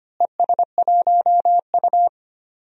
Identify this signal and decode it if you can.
Morse code, heard as a keyed tone, about 25 wpm, 700 Hz. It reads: ES1U
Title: DXCC-100 Call Signs - 4 Characters - Repeated 3x - 25wpm